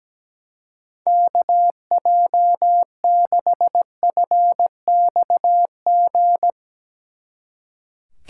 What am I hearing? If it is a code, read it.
Morse code, heard as a keyed tone, about 17 wpm, 700 Hz. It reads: KJ6FXG
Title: Speed Racing - USA 2x3 Call Signs 17wpm